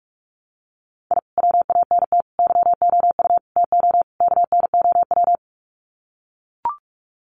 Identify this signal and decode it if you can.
Morse code, heard as a keyed tone, about 44 wpm, 700 Hz. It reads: I WANT YOU TO KNOW
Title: Sets of 5 Words Encore 45wpm